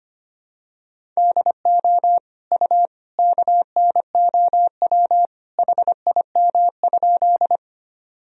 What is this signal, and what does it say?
Morse code, heard as a keyed tone, about 25 wpm, 700 Hz. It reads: DO U KNOW HIM?
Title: Sentences from Top 100 Words and Abbreviations 25wpm